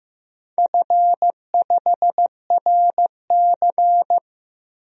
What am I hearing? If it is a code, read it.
Morse code, heard as a keyed tone, about 15 wpm, 700 Hz. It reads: F5RC
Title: DXCC-100 Call Signs - 4 Characters - Repeated 2x - 15wpm